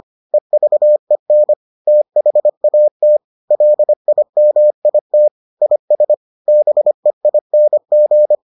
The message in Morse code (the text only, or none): EVEN THAT LIMIT IS BEING